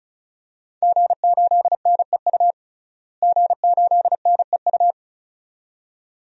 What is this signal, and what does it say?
Morse code, heard as a keyed tone, about 36 wpm, 700 Hz. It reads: G8NEU G8NEU
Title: DXCC-100 Call Signs - 5 Characters - Repeated 2x - 35wpm